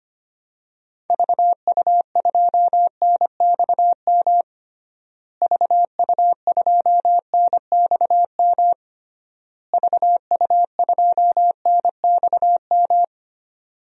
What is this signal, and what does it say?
Morse code, heard as a keyed tone, about 25 wpm, 700 Hz. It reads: VU2NXM VU2NXM VU2NXM
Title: DXCC-100 Call Signs - 6 Characters - Repeated 3x - 25wpm